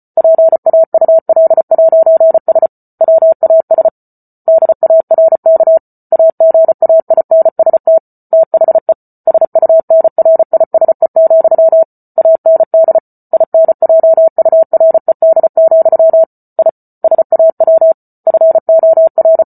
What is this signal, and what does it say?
Morse code, heard as a keyed tone, about 34 wpm, 650 Hz. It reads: PAUL'S WAS DARK AGAINST THE SUNRISE, AND INJURED, I SAW FOR
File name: Book Two / 08-Chapter 8 - 34 WPM